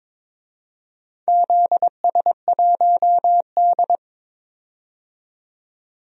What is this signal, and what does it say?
Morse code, heard as a keyed tone, about 22 wpm, 700 Hz. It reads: ZS1D
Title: DXCC-100 Call Signs - 4 Characters - 22wpm